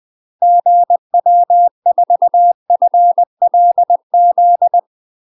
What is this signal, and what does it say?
Morse code, heard as a keyed tone, about 20 wpm, 700 Hz. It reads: GW4FLZ
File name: UK MASTER QSOs TRAINING / 20WPM / UK MASTER QSOs 0026